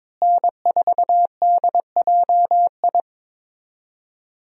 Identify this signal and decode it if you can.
Morse code, heard as a keyed tone, about 22 wpm, 700 Hz. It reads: N4DJI
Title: USA Call Signs - 5 Characters - Repeated 3x - 22wpm